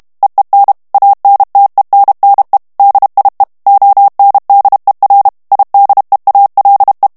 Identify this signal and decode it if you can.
Morse code, heard as a keyed tone, about 32 wpm, 800 Hz. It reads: EEN ANTENNE DIE ONDER IDEALE
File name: HOLLANDAIS texte clair 1 vitesse 30 mots minute